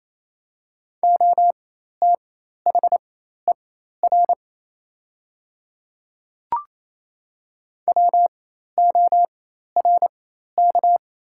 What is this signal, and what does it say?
Morse code, heard as a keyed tone, about 28 wpm, spaced as for 13 wpm, 700 Hz. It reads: OTHER WORK
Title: 4x Character Spacing - Top 200 Words - 28wpm